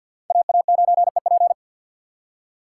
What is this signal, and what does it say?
Morse code, heard as a keyed tone, about 53 wpm, 700 Hz. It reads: AA9EP
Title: USA Call Signs - 5 Characters - 2x Character Spacing - 50wpm